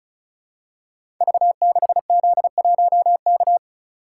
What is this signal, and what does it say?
Morse code, heard as a keyed tone, about 36 wpm, 700 Hz. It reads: V6Z1K
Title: Code Groups - 5 Letter:Numbers - 35wpm